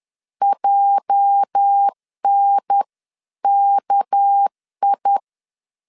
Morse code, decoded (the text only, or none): JNKI